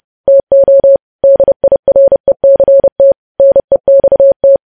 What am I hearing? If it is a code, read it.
Morse code, heard as a keyed tone, about 30 wpm, 550 Hz. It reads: TO DIRECT NEXT
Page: News Headlines in Morse Code at 30 WPM